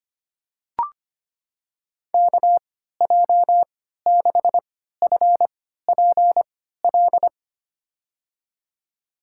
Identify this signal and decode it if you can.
Morse code, heard as a keyed tone, about 25 wpm, spaced as for 14 wpm, 700 Hz. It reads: KJ6FPL
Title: USA Call Signs - 3x Character Spacing - 25wpm